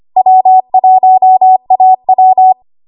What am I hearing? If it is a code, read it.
Morse code, heard as a keyed tone, about 25 wpm, 750 Hz. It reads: W1AW
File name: Morse / 25WPM